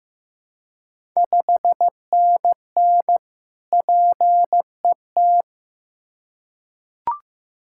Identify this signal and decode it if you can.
Morse code, heard as a keyed tone, about 15 wpm, 700 Hz. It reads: 5NN PET
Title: Ontario QSO Party Exchange 15wpm